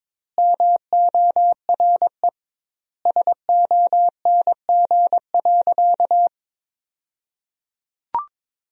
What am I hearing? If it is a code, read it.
Morse code, heard as a keyed tone, about 22 wpm, 700 Hz. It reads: MORE SONG.
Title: 2x Word-Spacing - Sentences from Top 400 Words - 22wpm